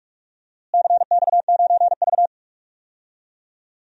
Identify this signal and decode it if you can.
Morse code, heard as a keyed tone, about 46 wpm, 700 Hz. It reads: CX9V
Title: DXCC-100 Call Signs - 4 Characters - Repeated 2x - 45wpm